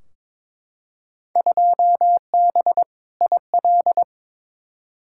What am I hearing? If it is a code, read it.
Morse code, heard as a keyed tone, about 22 wpm, 700 Hz. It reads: 2B IL
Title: Field Day Exchanges Encore - 22wpm